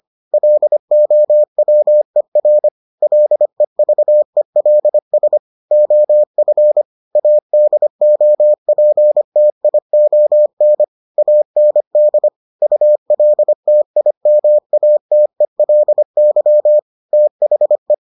Morse code, heard as a keyed tone, about 25 wpm, 600 Hz. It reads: LOWER LEVELS OF ADOPTION AND ULTIMATELY THE